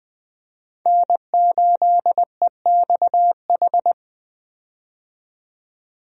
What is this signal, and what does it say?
Morse code, heard as a keyed tone, about 20 wpm, 700 Hz. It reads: N8EXH
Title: USA Call Signs - 5 Characters - 20wpm